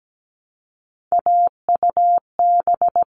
Morse code, heard as a keyed tone, about 17 wpm, 700 Hz. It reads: AUB
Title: All Cut Numbers - 3-Digit 17wpm